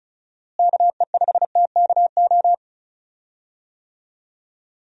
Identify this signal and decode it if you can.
Morse code, heard as a keyed tone, about 36 wpm, 700 Hz. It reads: KE5TKO
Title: USA Call Signs - 6 Characters - Repeated 2x - 35wpm